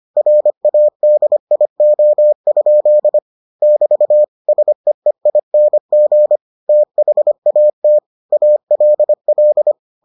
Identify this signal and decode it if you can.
Morse code, heard as a keyed tone, about 25 wpm, 600 Hz. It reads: RADIO? <BT> SEEING THAT ALL